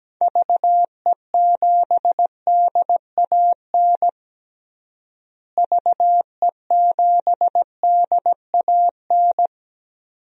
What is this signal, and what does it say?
Morse code, heard as a keyed tone, about 17 wpm, 700 Hz. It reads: VE7DAN VE7DAN
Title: DXCC-100 Call Signs - 6 Characters - Repeated 2x - 17wpm